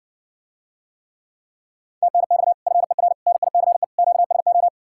Morse code, heard as a keyed tone, about 69 wpm, 700 Hz. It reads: TNX FER NICE QSO